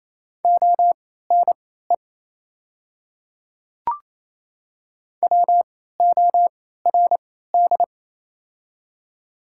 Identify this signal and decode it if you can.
Morse code, heard as a keyed tone, about 28 wpm, spaced as for 16 wpm, 690 Hz. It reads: ONE WORD
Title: 3x Character Spacing - Top 100 Words - 28wpm